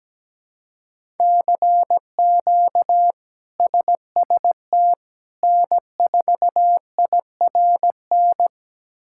Morse code, heard as a keyed tone, about 17 wpm, 700 Hz. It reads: CQ SST N4IRN
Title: K1USN SST QSO Practice - 17wpm